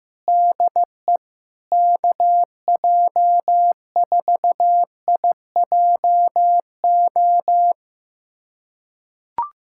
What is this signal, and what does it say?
Morse code, heard as a keyed tone, about 15 wpm, 700 Hz. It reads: DE KJ4IJO
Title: USA Callsign DE Callsign - 6 Characters - 15wpm